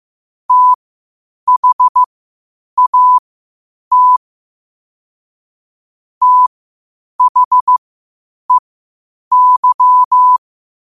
Morse code, heard as a keyed tone, about 15 wpm, spaced as for 8 wpm, 1000 Hz. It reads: THAT THEY
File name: qotdfiles / 8wpm